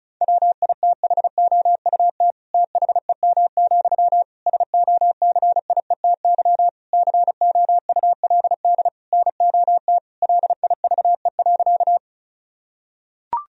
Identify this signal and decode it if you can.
Morse code, heard as a keyed tone, about 36 wpm, 700 Hz. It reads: WITHOUT THEM, SOCIETY COULD NOT LIVE.
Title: Sentences from Top 1000 Words - Review Part 4 - 35wpm